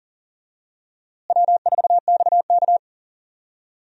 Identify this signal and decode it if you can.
Morse code, heard as a keyed tone, about 39 wpm, 700 Hz. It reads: W4XK
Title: USA Call Signs - 4 Characters - Repeated 3x - 40wpm